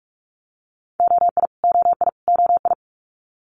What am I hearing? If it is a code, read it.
Morse code, heard as a keyed tone, about 46 wpm, 700 Hz. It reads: OI OI OI
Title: Multi-letter Phonemes Repeated 3x 45wpm